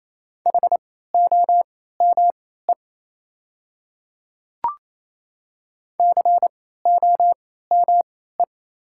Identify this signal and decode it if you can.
Morse code, heard as a keyed tone, about 28 wpm, spaced as for 16 wpm, 700 Hz. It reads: HOME COME
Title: 3x Character Spacing - Top 200 Words - 28wpm